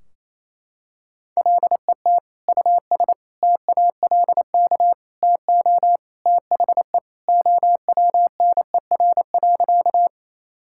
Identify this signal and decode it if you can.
Morse code, heard as a keyed tone, about 28 wpm, 700 Hz. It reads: LET US TALK TO THE OWNER.
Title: Sentences from Top 1600 Words 28wpm